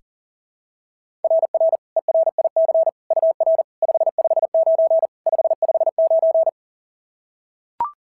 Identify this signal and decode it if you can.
Morse code, heard as a keyed tone, about 37 wpm, 690 Hz. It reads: RR ERIC UR 559 559
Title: SOTA QSO Practice 40wpm